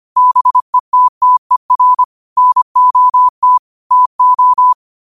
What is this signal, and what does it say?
Morse code, heard as a keyed tone, about 25 wpm, 1000 Hz. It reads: BETTER NOT TO
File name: qotdfiles / 25wpm